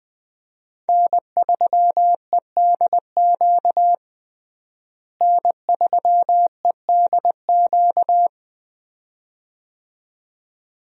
Morse code, heard as a keyed tone, about 20 wpm, 700 Hz. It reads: N3EDQ N3EDQ
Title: USA Call Signs - 5 Characters - Repeated 2x - 20wpm